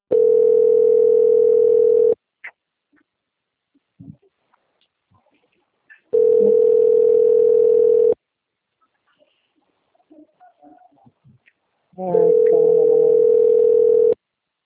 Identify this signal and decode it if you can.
Ringback tone (North America), 440 + 480 Hz.
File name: out- - - - - .38224